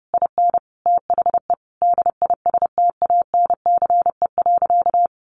Morse code, heard as a keyed tone, about 30 wpm, 700 Hz. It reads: IN THE DISTANCE.
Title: Sentences from Top 1500 Words 30wpm